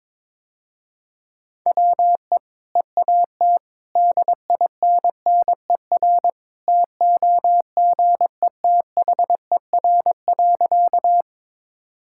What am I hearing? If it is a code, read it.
Morse code, heard as a keyed tone, about 22 wpm, 700 Hz. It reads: WE EAT DINNER TOGETHER.
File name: practice / 22wpm / Easy Sentences with Hints - 22wpm